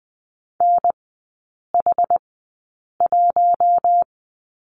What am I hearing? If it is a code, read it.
Morse code, heard as a keyed tone, about 20 wpm, spaced as for 8 wpm, 700 Hz. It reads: NH1
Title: 093 Lesson 16e - Letter c - Three Characters 20wpm